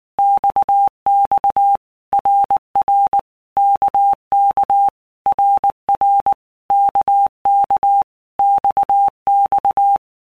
Morse code, heard as a keyed tone, about 19 wpm, 800 Hz. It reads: XX RR KK RR KK XX
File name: All Groups 1-minute Blocks / Groups@18 / RKPX